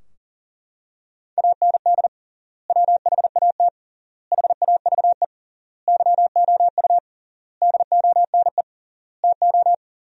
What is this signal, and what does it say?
Morse code, heard as a keyed tone, about 40 wpm, 700 Hz. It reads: AND WHAT HAVE YOU DONE TO
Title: 3x Word-Spacing - Sentences from Top 500 Words - 40wpm